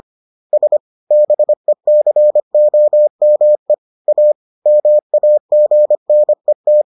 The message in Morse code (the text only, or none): S BECOME A MAGNET